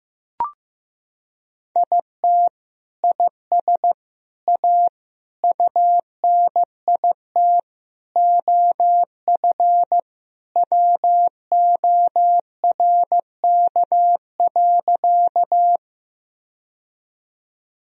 Morse code, heard as a keyed tone, about 15 wpm, 700 Hz. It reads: IT IS A UNIT OF WORK.
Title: Sentences from Top 1000 Words - Review Part 2 - 15wpm